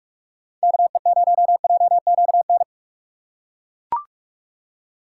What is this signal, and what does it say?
Morse code, heard as a keyed tone, about 43 wpm, 700 Hz. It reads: KE0JQN
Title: USA Call Signs - 6 Characters - 3x Character Spacing - 45wpm